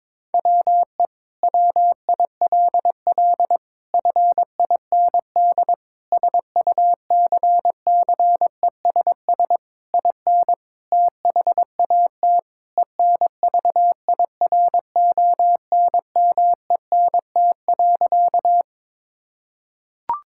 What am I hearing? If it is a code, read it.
Morse code, heard as a keyed tone, about 22 wpm, 700 Hz. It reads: WE WILL FIND SUCCESS IN THAT ENVIRONMENT.
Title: Sentences from Top 1000 Words - Review Part 5 - 22wpm